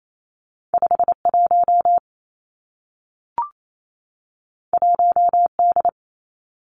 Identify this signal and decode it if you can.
Morse code, heard as a keyed tone, about 28 wpm, 700 Hz. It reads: 5 1 1 D
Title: Two Challenging Letter-Number Combinations 28wpm